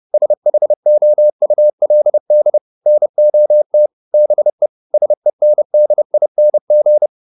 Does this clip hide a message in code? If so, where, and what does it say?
Yes: Morse code from 0.1 to 7.1 s, heard as a keyed tone, about 30 wpm, 600 Hz. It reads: SHOULD NOT BE SENDING